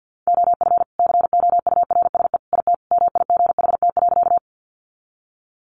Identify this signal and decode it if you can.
Morse code, heard as a keyed tone, about 50 wpm, 700 Hz. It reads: OF COURSE IT MIGHT.
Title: Sentences from Top 500 Words 50wpm